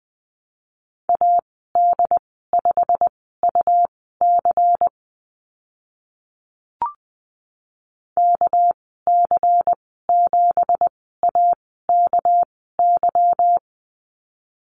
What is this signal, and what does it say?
Morse code, heard as a keyed tone, about 19 wpm, spaced as for 14 wpm, 700 Hz. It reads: AD5UC KC7AKY
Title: 238 Lesson 38g - Letter q - Call Signs 20wpm